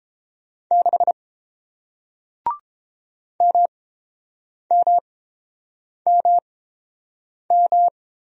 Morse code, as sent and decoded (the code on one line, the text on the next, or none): -.... / -- -- -- --
6 MMMM